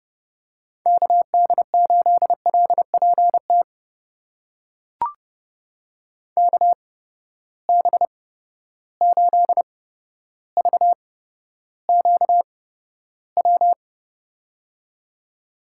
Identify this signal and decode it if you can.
Morse code, heard as a keyed tone, about 30 wpm, spaced as for 13 wpm, 700 Hz. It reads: KD8LPT KB8VQW
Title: USA Call Signs - 8x Character Spacing - 30wpm